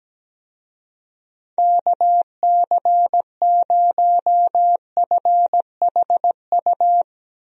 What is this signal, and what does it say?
Morse code, heard as a keyed tone, about 17 wpm, 700 Hz. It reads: KC0FHU
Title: USA Call Signs - 6 Characters - 17wpm